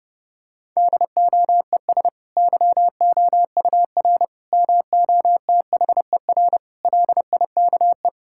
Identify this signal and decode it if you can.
Morse code, heard as a keyed tone, about 32 wpm, 700 Hz. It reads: DOES YOUR MOTHER LIKE
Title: Sentences from Top 1000 Words - Review Part 5 - 30wpm